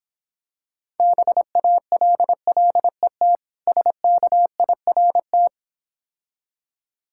Morse code, heard as a keyed tone, about 26 wpm, 690 Hz. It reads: BALLET SKIRT